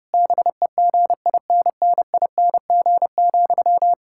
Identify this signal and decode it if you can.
Morse code, heard as a keyed tone, about 32 wpm, 700 Hz. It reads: BEGINNING,
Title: Sentences from Top 1000 Words - Review Part 6 - 30wpm